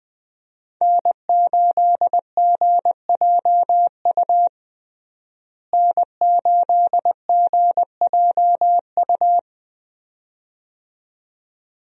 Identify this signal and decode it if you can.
Morse code, heard as a keyed tone, about 21 wpm, 700 Hz. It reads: N8GJU N8GJU
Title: USA Call Signs - 5 Characters - Repeated 2x - 20wpm